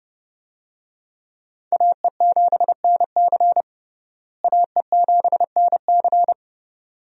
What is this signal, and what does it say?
Morse code, heard as a keyed tone, about 30 wpm, 700 Hz. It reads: AE7NC AE7NC
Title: USA Call Signs - 5 Characters - Repeated 2x - 30wpm